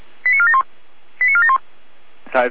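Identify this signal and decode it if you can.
Five-tone selective call (ZVEI), heard as quick runs of five tones, tones of about 70 ms. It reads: 89471 89471
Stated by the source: ZVEI five-tone call 89303 received by SDR.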